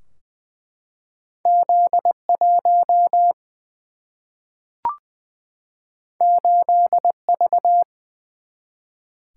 Morse code, heard as a keyed tone, about 20 wpm, 700 Hz. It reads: Z 1 8 V
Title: Single-Letter Single-Number 20wpm